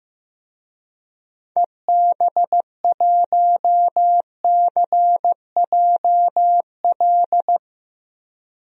Morse code, heard as a keyed tone, about 16 wpm, 700 Hz. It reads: EB1CJL
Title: DXCC-100 Call Signs - 6 Characters - 15wpm